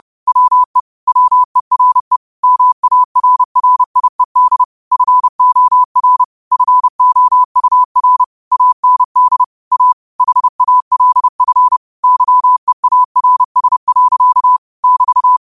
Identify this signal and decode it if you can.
Morse code, heard as a keyed tone, about 30 wpm, 1000 Hz. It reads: WE WERE MARRIED FOR FOUR AND A HALF YEARS. <BT>